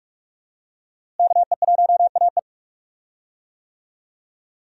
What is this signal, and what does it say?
Morse code, heard as a keyed tone, about 46 wpm, 700 Hz. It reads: KE1AE
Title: USA Call Signs - 5 Characters - Repeated 2x - 45wpm